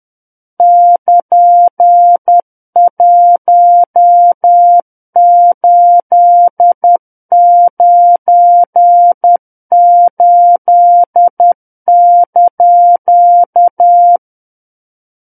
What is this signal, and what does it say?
Morse code, heard as a keyed tone, about 10 wpm, 700 Hz. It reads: <KN>1898)
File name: Book One / 01-Chapter 1 - 10 WPM